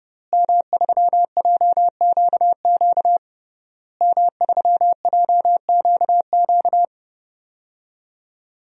Morse code, heard as a keyed tone, about 30 wpm, 700 Hz. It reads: M3JQQ M3JQQ
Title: DXCC-100 Call Signs - 5 Characters - Repeated 3x - 30wpm